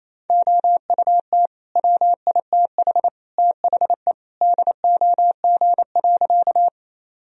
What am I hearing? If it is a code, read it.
Morse code, heard as a keyed tone, about 28 wpm, 700 Hz. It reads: OUT WITH THE DOG.
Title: Sentences from Top 1000 Words - Review Part 2 - 28wpm